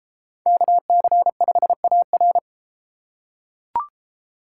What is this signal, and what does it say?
Morse code, heard as a keyed tone, about 32 wpm, 700 Hz. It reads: KC5AR